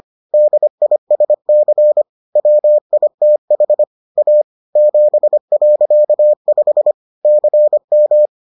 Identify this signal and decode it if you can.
Morse code, heard as a keyed tone, about 25 wpm, 600 Hz. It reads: DISC WITH A 7.5 CM